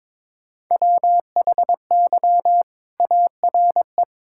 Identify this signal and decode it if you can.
Morse code, heard as a keyed tone, about 22 wpm, 700 Hz. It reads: WHY ARE
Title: Sentences from Top 1000 Words - Review Part 5 - 22wpm